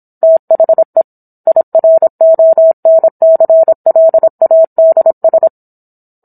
Morse code, heard as a keyed tone, about 27 wpm, 650 Hz. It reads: THE IRONCLADS